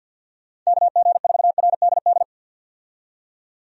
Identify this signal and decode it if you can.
Morse code, heard as a keyed tone, about 49 wpm, 700 Hz. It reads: KG4RDD
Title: USA Call Signs - 6 Characters - 50wpm